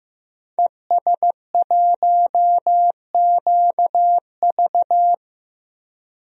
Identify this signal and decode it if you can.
Morse code, heard as a keyed tone, about 15 wpm, 700 Hz. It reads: ES1QV
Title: DXCC-100 Call Signs - 5 Characters - Repeated 3x - 15wpm